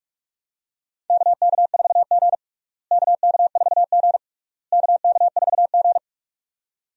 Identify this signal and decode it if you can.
Morse code, heard as a keyed tone, about 46 wpm, 700 Hz. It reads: KK4G KK4G KK4G
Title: USA Call Signs - 4 Characters - Repeated 3x - 45wpm